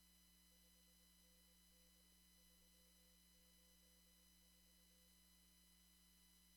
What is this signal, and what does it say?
Morse code, heard as a keyed tone, about 20 wpm, 510 Hz. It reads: 5NN 384
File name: CW / kt5j wpx cw-op-k5pi